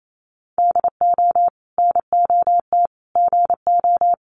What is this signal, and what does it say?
Morse code, heard as a keyed tone, about 28 wpm, 700 Hz. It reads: DO NOT GO
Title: Sentences from Top 1500 Words 28wpm